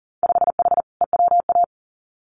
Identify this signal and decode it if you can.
Morse code, heard as a keyed tone, about 39 wpm, 700 Hz. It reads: 5H EWA
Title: Winter Field Day Exchanges Encore 40wpm